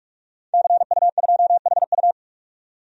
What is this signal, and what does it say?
Morse code, heard as a keyed tone, about 47 wpm, 700 Hz. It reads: CU2HU
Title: DXCC-100 Call Signs - 5 Characters - 45wpm